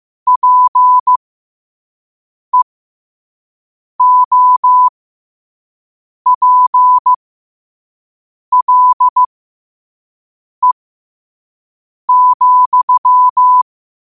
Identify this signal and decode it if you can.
Morse code, heard as a keyed tone, about 15 wpm, spaced as for 5 wpm, 1000 Hz. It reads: PEOPLE,
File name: qotdfiles / 5wpm